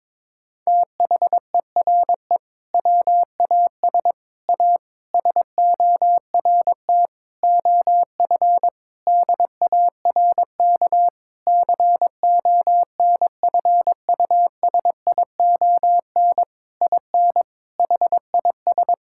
Morse code, heard as a keyed tone, about 22 wpm, 700 Hz. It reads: THERE WAS A SORT OF DARK CONFUSION IN HIS
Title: Sentences from Top 1600 Words 22wpm